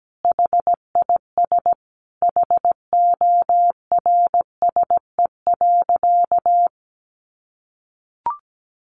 Morse code, heard as a keyed tone, about 17 wpm, 700 Hz. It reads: HIS HORSE.
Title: Sentences from Top 300 Words 17wpm